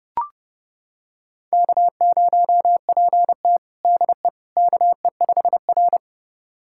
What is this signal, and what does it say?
Morse code, heard as a keyed tone, about 30 wpm, 700 Hz. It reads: K0PT DE KE5R
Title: USA Callsign DE Callsign - 4 Characters - 30wpm